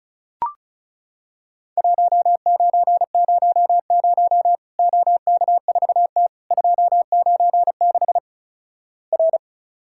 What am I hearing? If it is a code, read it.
Morse code, heard as a keyed tone, about 36 wpm, 700 Hz. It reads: 1900 OK4T 296 R
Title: Worked All Europe DX Contest - QTC Exchange - 35wpm